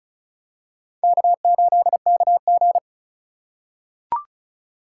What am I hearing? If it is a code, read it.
Morse code, heard as a keyed tone, about 34 wpm, 690 Hz. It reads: K8KG
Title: USA Call Signs - 4 Characters - 3x Character Spacing - 35wpm